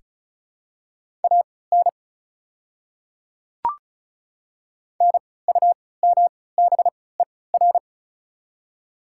Morse code, heard as a keyed tone, about 36 wpm, spaced as for 20 wpm, 690 Hz. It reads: AN NUMBER
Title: 3x Character Spacing - Top 100 Words - 35wpm